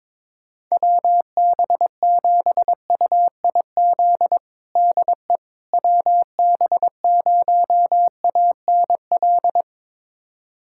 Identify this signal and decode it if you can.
Morse code, heard as a keyed tone, about 22 wpm, 700 Hz. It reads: WB7UIZ DE WB0ANL
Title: USA Callsign DE Callsign - 6 Characters - 22wpm